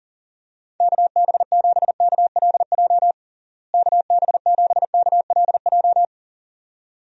Morse code, heard as a keyed tone, about 40 wpm, 700 Hz. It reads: KB7KLJ KB7KLJ
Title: USA Call Signs - 6 Characters - Repeated 2x - 40wpm